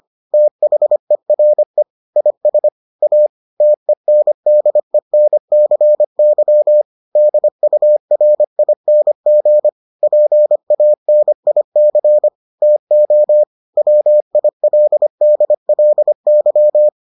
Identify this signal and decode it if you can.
Morse code, heard as a keyed tone, about 25 wpm, 600 Hz. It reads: THERE IS A TENDENCY DURING PANIC TO WILDLY